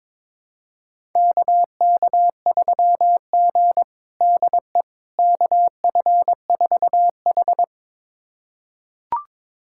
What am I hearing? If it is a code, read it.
Morse code, heard as a keyed tone, about 22 wpm, 700 Hz. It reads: KK3G DE KF4H
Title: USA Callsign DE Callsign - 4 Characters - 22wpm